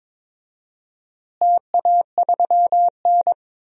Morse code, heard as a keyed tone, about 21 wpm, 700 Hz. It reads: TA3N
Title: DXCC-100 Call Signs - 4 Characters - Repeated 2x - 22wpm